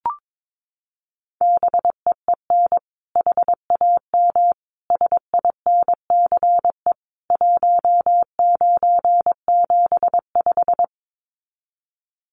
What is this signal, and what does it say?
Morse code, heard as a keyed tone, about 22 wpm, 700 Hz. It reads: BEEN HAM SINCE 1975
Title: Common QSO Phrases 22wpm